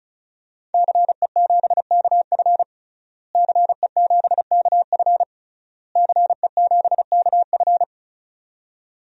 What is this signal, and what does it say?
Morse code, heard as a keyed tone, about 36 wpm, 700 Hz. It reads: CE7KF CE7KF CE7KF
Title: DXCC-100 Call Signs - 5 Characters - Repeated 3x - 35wpm